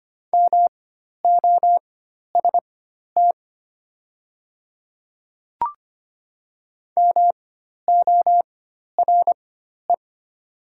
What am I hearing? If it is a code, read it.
Morse code, heard as a keyed tone, about 25 wpm, spaced as for 12 wpm, 700 Hz. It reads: MOST MORE
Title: 4x Character Spacing - Top 300 Words - 25wpm